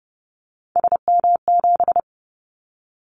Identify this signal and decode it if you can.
Morse code, heard as a keyed tone, about 30 wpm, 700 Hz. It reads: SM7
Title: Callsign Prefixes for Top 25 DXCC - 30wpm